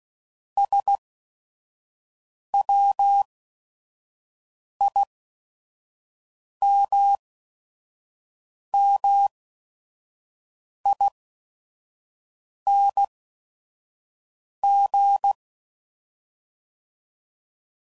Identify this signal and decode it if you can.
Morse code, heard as a keyed tone, about 16 wpm, spaced as for 5 wpm, 800 Hz. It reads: SWIMMING
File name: Suffixes / Special ING's / VerySlowToFaster ING's